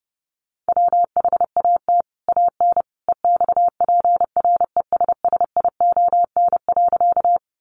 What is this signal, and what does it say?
Morse code, heard as a keyed tone, about 30 wpm, 700 Hz. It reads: WHAT AN EXPRESSION.
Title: Sentences from Top 1200 Words 30wpm